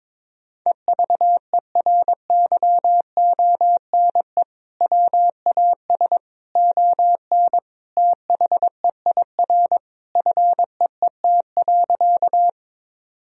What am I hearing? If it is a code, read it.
Morse code, heard as a keyed tone, about 22 wpm, 700 Hz. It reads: EVERYONE WAS ON THEIR FEET.
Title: Sentences from Top 1000 Words - Review Part 1 - 22wpm